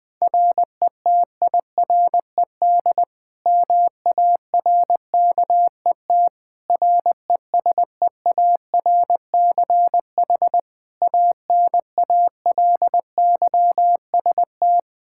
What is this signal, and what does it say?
Morse code, heard as a keyed tone, about 20 wpm, 700 Hz. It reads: RETIRED MARKET RESEARCH ANALYST
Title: Practice QSO - Exchange 3 - More Info - 20wpm